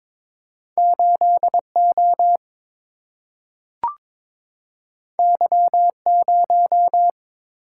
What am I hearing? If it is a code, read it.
Morse code, heard as a keyed tone, about 22 wpm, 700 Hz. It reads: 8 O Y 0
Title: Single-Letter Single-Number 22wpm